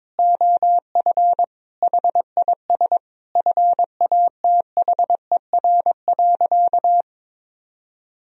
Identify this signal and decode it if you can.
Morse code, heard as a keyed tone, about 22 wpm, 700 Hz. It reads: OF HIS FATHER.
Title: Sentences from Top 1000 Words - Review Part 1 - 22wpm